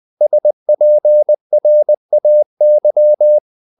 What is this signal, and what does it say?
Morse code, heard as a keyed tone, about 20 wpm, 600 Hz. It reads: SPRAY
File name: the old man and the sea part 2 20wpm